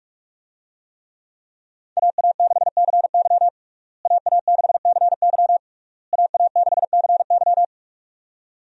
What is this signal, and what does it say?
Morse code, heard as a keyed tone, about 46 wpm, 700 Hz. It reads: AA6CY AA6CY AA6CY
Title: USA Call Signs - 5 Characters - Repeated 3x - 45wpm